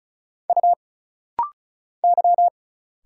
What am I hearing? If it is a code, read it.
Morse code, heard as a keyed tone, about 34 wpm, 700 Hz. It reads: U Y
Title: Single Letters - Rapid-Fire 35wpm